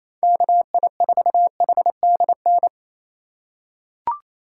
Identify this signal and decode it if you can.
Morse code, heard as a keyed tone, about 28 wpm, 700 Hz. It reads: KI4HDN
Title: USA Call Signs - 6 Characters - 4x Character Spacing - 28wpm